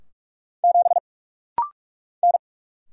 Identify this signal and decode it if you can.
Morse code, heard as a keyed tone, about 44 wpm, 700 Hz. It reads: 7 N
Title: Single Letter-Number - Rapid-Fire 45wpm